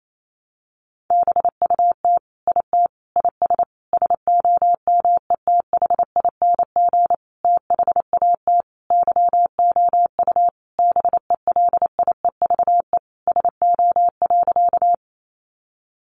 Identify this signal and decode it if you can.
Morse code, heard as a keyed tone, about 28 wpm, 700 Hz. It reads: BUT IT IS SOMETHING THAT YOU BELIEVE SO.
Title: Sentences from Top 1000 Words 28wpm